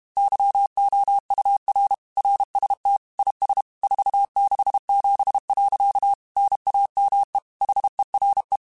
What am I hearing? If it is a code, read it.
Morse code, heard as a keyed tone, about 32 wpm, 800 Hz. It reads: YOUR RST IS 467. NAME HERE
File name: QSOs / QSOs-mp3s / WA7QOX 2@30